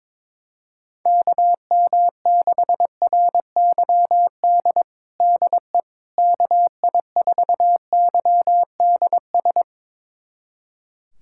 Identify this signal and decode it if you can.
Morse code, heard as a keyed tone, about 22 wpm, 700 Hz. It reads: KM6RYD DE KI4YDS
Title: USA Callsign DE Callsign - 6 Characters - 22wpm